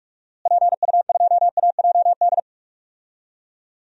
Morse code, heard as a keyed tone, about 44 wpm, 700 Hz. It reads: PU2AJD
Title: DXCC-100 Call Signs - 6 Characters - Repeated 3x - 45wpm